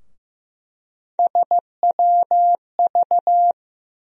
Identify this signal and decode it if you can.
Morse code, heard as a keyed tone, about 16 wpm, 700 Hz. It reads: SWV
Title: USA Call Sign Postfixes - 3 Letters 15wpm